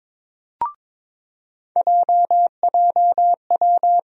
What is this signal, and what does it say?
Morse code, heard as a keyed tone, about 22 wpm, 700 Hz. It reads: JJW
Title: USA Call Sign Postfixes - 3 Letters 22wpm